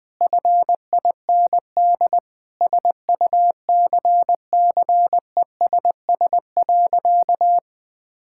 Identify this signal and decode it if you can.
Morse code, heard as a keyed tone, about 21 wpm, 700 Hz. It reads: FIND SUCCESS.
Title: Sentences from Top 1000 Words - Review Part 6 - 20wpm